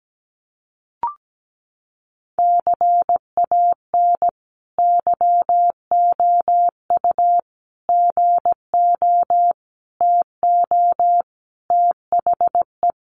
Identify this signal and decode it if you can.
Morse code, heard as a keyed tone, about 17 wpm, 700 Hz. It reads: CAN YOU GO TO THE
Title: Sentences from Top 1500 Words 17wpm